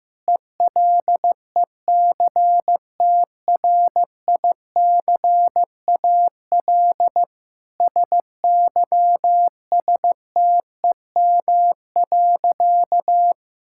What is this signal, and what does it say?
Morse code, heard as a keyed tone, about 15 wpm, 700 Hz. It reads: ELECTRICAL SYSTEM.
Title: Sentences from Top 1000 Words - Review Part 6 - 15wpm